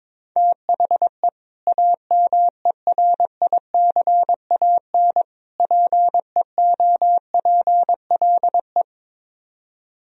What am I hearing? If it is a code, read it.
Morse code, heard as a keyed tone, about 22 wpm, 700 Hz. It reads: THE AMERICAN PEOPLE
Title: Speed-Racing - Sets of 3 Words 22wpm